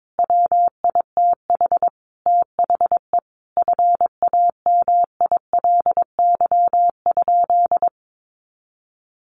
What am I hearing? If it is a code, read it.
Morse code, heard as a keyed tone, about 22 wpm, 700 Hz. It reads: WITH THE FAMILY?
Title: Sentences from Top 1100 Words 22wpm